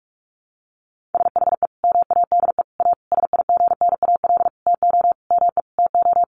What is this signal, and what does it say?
Morse code, heard as a keyed tone, about 44 wpm, 700 Hz. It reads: SHE MADE A SIGNAL TO ME TO
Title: Sentences from Top 1400 Words 45wpm